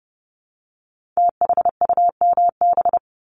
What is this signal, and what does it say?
Morse code, heard as a keyed tone, about 30 wpm, 700 Hz. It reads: THUMB
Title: Multi-letter Phonemes 30wpm